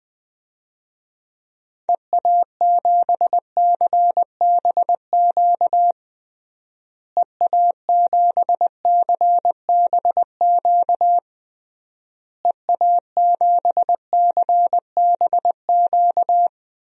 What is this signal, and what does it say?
Morse code, heard as a keyed tone, about 20 wpm, 700 Hz. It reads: EA7CBQ EA7CBQ EA7CBQ
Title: DXCC-100 Call Signs - 6 Characters - Repeated 3x - 20wpm